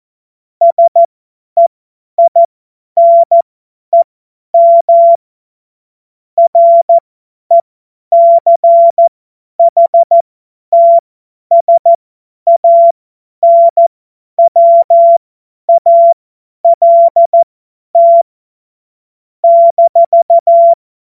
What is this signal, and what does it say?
Morse code, heard as a keyed tone, about 14 wpm, spaced as for 10 wpm, 680 Hz. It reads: SEINEM RECHTSANWALT -
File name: Morsekurs / Witze / Witz 03. Mai 2022 50 F70